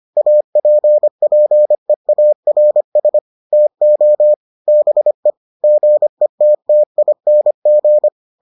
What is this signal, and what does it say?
Morse code, heard as a keyed tone, about 25 wpm, 600 Hz. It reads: APPEARS TO BE GETTING